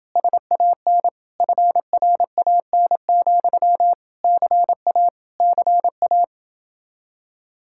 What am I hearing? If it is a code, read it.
Morse code, heard as a keyed tone, about 29 wpm, 700 Hz. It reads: SAN FRAN, CA CA